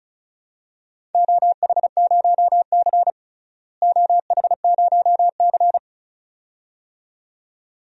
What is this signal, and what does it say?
Morse code, heard as a keyed tone, about 34 wpm, 700 Hz. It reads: OH0C OH0C
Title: DXCC-100 Call Signs - 4 Characters - Repeated 2x - 35wpm